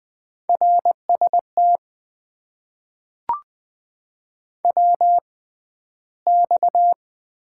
Morse code, heard as a keyed tone, about 20 wpm, spaced as for 12 wpm, 700 Hz. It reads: RST WX
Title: Top 90 QSO Elements - 6x Character Spacing - 20wpm